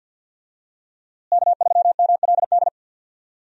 Morse code, heard as a keyed tone, about 52 wpm, 700 Hz. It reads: K3MLD
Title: USA Call Signs - 5 Characters - Repeated 2x - 50wpm